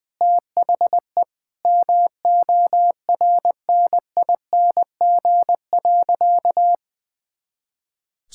Morse code, heard as a keyed tone, about 20 wpm, 700 Hz. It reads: THE MORNING.
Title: Easy Words with Hints - 20wpm